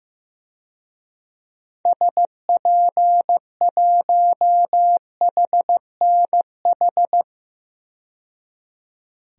Morse code, heard as a keyed tone, about 15 wpm, 700 Hz. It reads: SP1HNH
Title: DXCC-100 Call Signs - 6 Characters - 15wpm